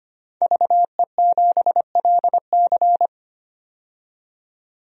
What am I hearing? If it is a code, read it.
Morse code, heard as a keyed tone, about 25 wpm, 700 Hz. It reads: VE7LC
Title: DXCC-100 Call Signs - 5 Characters - 25wpm